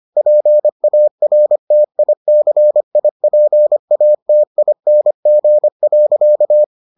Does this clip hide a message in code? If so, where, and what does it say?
Yes: Morse code from 0.2 to 6.6 s, heard as a keyed tone, about 25 wpm, 600 Hz. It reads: PARTICIPATING.